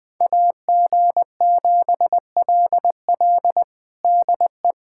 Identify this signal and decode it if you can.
Morse code, heard as a keyed tone, about 20 wpm, 700 Hz. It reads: AG7LL DE
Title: USA Callsign DE Callsign - 5 Characters - 20wpm